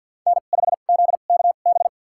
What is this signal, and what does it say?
Morse code, heard as a keyed tone, about 53 wpm, spaced as for 36 wpm, 690 Hz. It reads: N5ZKB
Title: USA Call Signs - 2x Character Spacing - 50wpm